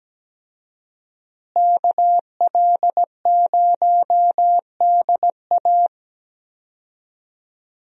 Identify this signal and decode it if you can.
Morse code, heard as a keyed tone, about 17 wpm, 700 Hz. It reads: KL0DA
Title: USA Call Signs - 5 Characters - Repeated 2x - 17wpm